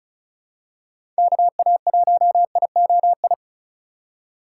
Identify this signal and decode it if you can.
Morse code, heard as a keyed tone, about 36 wpm, 700 Hz. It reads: KA1IOI
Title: USA Call Signs - 6 Characters - 4x Character Spacing - 35wpm